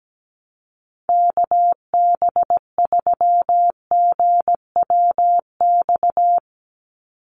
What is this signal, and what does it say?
Morse code, heard as a keyed tone, about 17 wpm, 700 Hz. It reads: KB3GWX
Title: USA Call Signs 17wpm